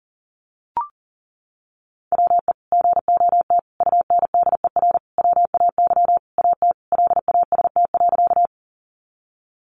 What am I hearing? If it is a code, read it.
Morse code, heard as a keyed tone, about 40 wpm, 700 Hz. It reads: WE GOT UNDER WAY AT LAST.
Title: Sentences from Top 300 Words 40wpm